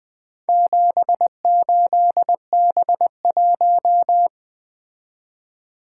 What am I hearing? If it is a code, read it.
Morse code, heard as a keyed tone, about 20 wpm, 700 Hz. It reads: 78B1
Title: Four Letter-Number Combinations - 20wpm